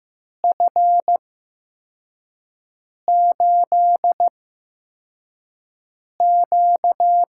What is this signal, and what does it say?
Morse code, heard as a keyed tone, about 15 wpm, spaced as for 4 wpm, 700 Hz. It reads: F8Q
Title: USA Call Signs - 4 Characters - 8x Character Spacing - 15wpm